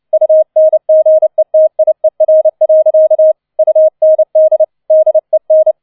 Morse code, heard as a keyed tone, about 29 wpm, 630 Hz. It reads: UNGETIER. UND DEN